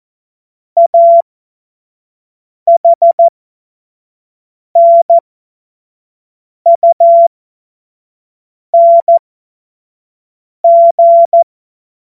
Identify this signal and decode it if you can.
Morse code, heard as a keyed tone, about 14 wpm, spaced as for 5 wpm, 680 Hz. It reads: AHNUNG
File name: Morsekurs / Witze / Witz 29. Juni 2021, 25 F70